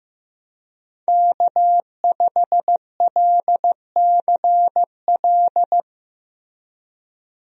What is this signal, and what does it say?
Morse code, heard as a keyed tone, about 15 wpm, 700 Hz. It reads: K5LCL
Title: USA Call Signs - 5 Characters - Repeated 3x - 15wpm